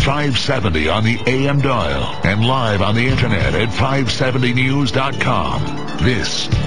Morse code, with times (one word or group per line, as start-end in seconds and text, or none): none